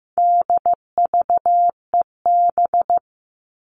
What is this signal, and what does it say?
Morse code, heard as a keyed tone, about 15 wpm, 700 Hz. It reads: DVEB
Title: All Cut Numbers - 4-Digit 15wpm